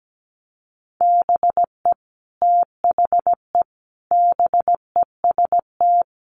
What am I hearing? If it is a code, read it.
Morse code, heard as a keyed tone, about 17 wpm, 700 Hz. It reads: BE THE BEST
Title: Sentences from Top 1400 Words 17wpm